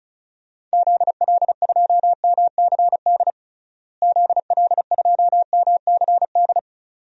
Morse code, heard as a keyed tone, about 36 wpm, 700 Hz. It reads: ZL2MCD ZL2MCD
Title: DXCC-100 Call Signs - 6 Characters - Repeated 2x - 35wpm